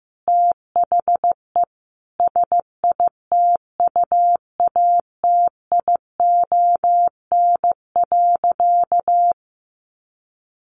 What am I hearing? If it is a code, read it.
Morse code, heard as a keyed tone, about 15 wpm, 700 Hz. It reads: THE SITUATION.
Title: Sentences from Top 1200 Words 15wpm